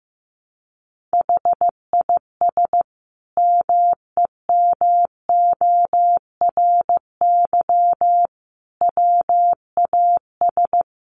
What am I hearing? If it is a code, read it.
Morse code, heard as a keyed tone, about 15 wpm, 700 Hz. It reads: HIS MEMORY WAS
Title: Sentences from Top 1300 Words 15wpm